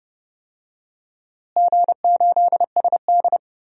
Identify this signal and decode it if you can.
Morse code, heard as a keyed tone, about 30 wpm, 700 Hz. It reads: G8SD
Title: DXCC-100 Call Signs - 4 Characters - Repeated 2x - 30wpm